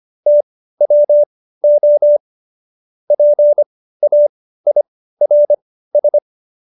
Morse code, heard as a keyed tone, about 25 wpm, spaced as for 15 wpm, 590 Hz. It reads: TWO PAIRS
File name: Jokes-25-char-15-eff